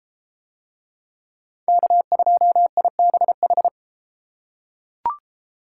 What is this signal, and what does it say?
Morse code, heard as a keyed tone, about 32 wpm, 700 Hz. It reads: K2IBH